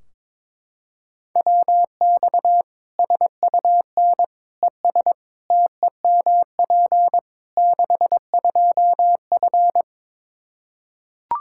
Morse code, heard as a keyed tone, about 22 wpm, 700 Hz. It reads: WX SUN ES TEMP 62F
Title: Practice QSO - Exchange 2 - Info - 22wpm